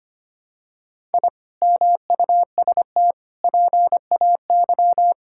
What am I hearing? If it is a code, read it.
Morse code, heard as a keyed tone, about 25 wpm, 700 Hz. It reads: I MUST PAY
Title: Sentences from Top 1000 Words - Review Part 6 - 25wpm